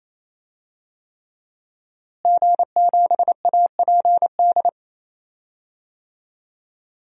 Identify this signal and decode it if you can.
Morse code, heard as a keyed tone, about 28 wpm, 700 Hz. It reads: G7APD
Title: DXCC-100 Call Signs - 5 Characters - Repeated 3x - 28wpm